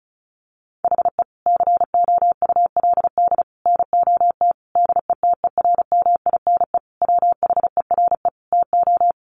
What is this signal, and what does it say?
Morse code, heard as a keyed tone, about 34 wpm, 700 Hz. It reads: HE COULD NOT DETERMINE WHERE TO
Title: Sentences from Top 700 Words 35wpm